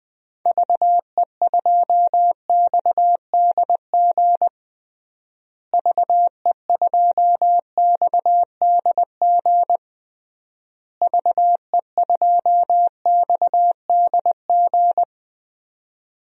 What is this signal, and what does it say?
Morse code, heard as a keyed tone, about 20 wpm, 700 Hz. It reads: VE2XDG VE2XDG VE2XDG
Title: DXCC-100 Call Signs - 6 Characters - Repeated 3x - 20wpm